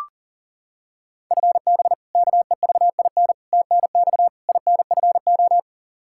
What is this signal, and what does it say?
Morse code, heard as a keyed tone, about 37 wpm, 700 Hz. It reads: FB KEVIN TNX INFO
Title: Practice QSO - Exchange 3 - More Info - 40wpm